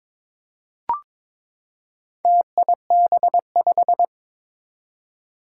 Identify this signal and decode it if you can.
Morse code, heard as a keyed tone, about 22 wpm, 700 Hz. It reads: TIB5
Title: Four Challenging Letter-Number Combinations 22wpm